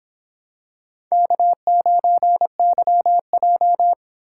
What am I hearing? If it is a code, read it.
Morse code, heard as a keyed tone, about 26 wpm, 700 Hz. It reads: K9YJ